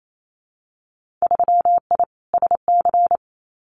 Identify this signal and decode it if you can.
Morse code, heard as a keyed tone, about 28 wpm, 700 Hz. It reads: 3I SC
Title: Winter Field Day Exchanges 28wpm